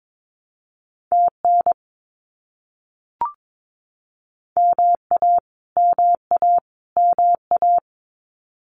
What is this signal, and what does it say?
Morse code, heard as a keyed tone, about 22 wpm, 700 Hz. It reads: TN MA MA MA
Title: US States Repeated 3x 22wpm